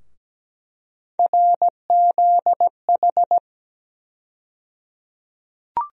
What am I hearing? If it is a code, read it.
Morse code, heard as a keyed tone, about 17 wpm, 700 Hz. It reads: RZH
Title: USA Call Sign Postfixes - 3 Letters 17wpm